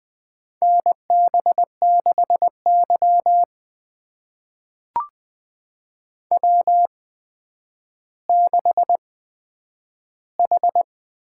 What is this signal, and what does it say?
Morse code, heard as a keyed tone, about 20 wpm, spaced as for 10 wpm, 700 Hz. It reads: NB6Y W6H
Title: USA Call Signs - 8x Character Spacing - 20wpm